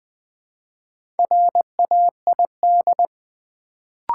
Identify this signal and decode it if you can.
Morse code, heard as a keyed tone, about 20 wpm, 700 Hz. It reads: RAID
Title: Build A Word - 2 Letters At A Time - 4-Letter Words - 20wpm